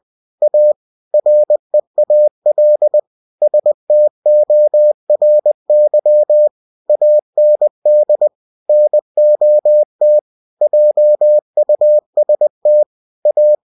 Morse code, heard as a keyed tone, about 20 wpm, 600 Hz. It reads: A REAL STORY AND NOT JUST A
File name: cw-books / winnie-the-pooh / winnie-20wpm-page-0011